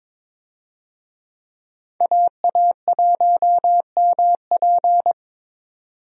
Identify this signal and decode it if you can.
Morse code, heard as a keyed tone, about 22 wpm, 700 Hz. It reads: AA1MP
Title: USA Call Signs - 5 Characters - 22wpm